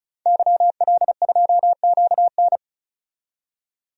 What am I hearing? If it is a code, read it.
Morse code, heard as a keyed tone, about 36 wpm, 700 Hz. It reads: YL2QN
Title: DXCC-100 Call Signs - 5 Characters - 35wpm